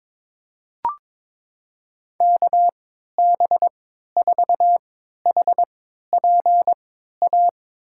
Morse code, heard as a keyed tone, about 22 wpm, spaced as for 12 wpm, 700 Hz. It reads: KB4HPA
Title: USA Call Signs - 3x Character Spacing - 22wpm